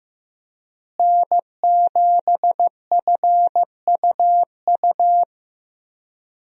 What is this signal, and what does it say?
Morse code, heard as a keyed tone, about 16 wpm, 700 Hz. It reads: N7FUU
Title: USA Call Signs - 5 Characters - Repeated 3x - 15wpm